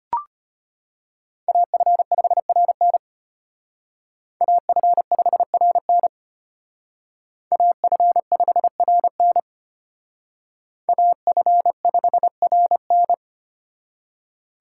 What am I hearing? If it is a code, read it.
Morse code, heard as a keyed tone, about 32 wpm, 700 Hz. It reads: AF5RN AF5RN AF5RN AF5RN